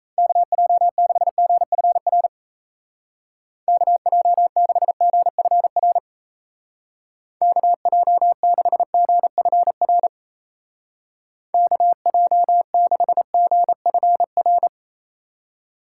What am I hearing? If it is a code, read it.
Morse code, heard as a keyed tone, about 35 wpm, 700 Hz. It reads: KJ6GFR KJ6GFR KJ6GFR KJ6GFR
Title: Speed Racing - USA 2x3 Call Signs 28wpm